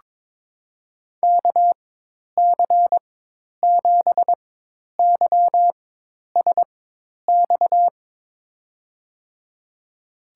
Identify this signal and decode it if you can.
Morse code, heard as a keyed tone, about 22 wpm, spaced as for 10 wpm, 700 Hz. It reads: KC7YSX
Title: USA Call Signs - 4x Character Spacing - 22wpm